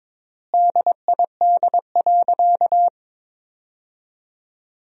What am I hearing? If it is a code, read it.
Morse code, heard as a keyed tone, about 22 wpm, 700 Hz. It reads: DID.
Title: 4x Word-Spacing - Sentences from Top 100 Words - 22wpm